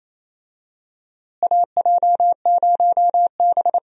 Morse code, heard as a keyed tone, about 28 wpm, 700 Hz. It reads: AJ0B
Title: USA Call Signs - 4 Characters - Repeated 3x - 28wpm